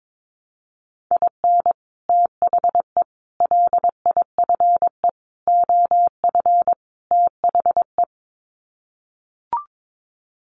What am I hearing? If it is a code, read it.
Morse code, heard as a keyed tone, about 22 wpm, 700 Hz. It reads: IN THE LIFE OF THE
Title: Sets of 5 Words Encore2 22wpm